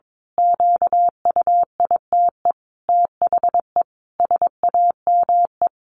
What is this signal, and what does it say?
Morse code, heard as a keyed tone, about 22 wpm, 700 Hz. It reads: QUITE THE SAME
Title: A Sound Of Thunder 22wpm